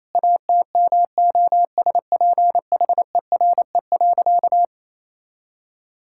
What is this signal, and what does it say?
Morse code, heard as a keyed tone, about 28 wpm, 700 Hz. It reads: ATMOSPHERE.
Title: Sentences from Top 1600 Words 28wpm